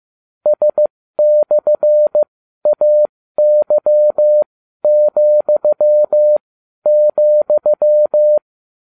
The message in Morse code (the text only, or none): S/AY,,